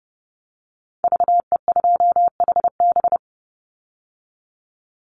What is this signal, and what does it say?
Morse code, heard as a keyed tone, about 30 wpm, 700 Hz. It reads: VE2HB
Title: Call Signs - 30wpm